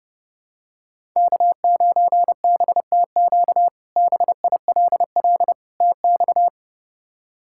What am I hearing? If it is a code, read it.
Morse code, heard as a keyed tone, about 30 wpm, 700 Hz. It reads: K9BTQ BILL TX
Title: K1USN SST QSO Practice - 30wpm